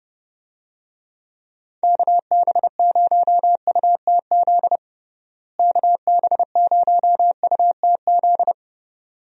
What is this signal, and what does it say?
Morse code, heard as a keyed tone, about 30 wpm, 700 Hz. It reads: KB0UTZ KB0UTZ
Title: USA Call Signs - 6 Characters - Repeated 3x - 30wpm